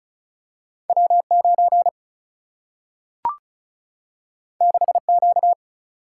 Morse code, heard as a keyed tone, about 36 wpm, 700 Hz. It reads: W9 6Q
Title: Single-Letter Single-Number 35wpm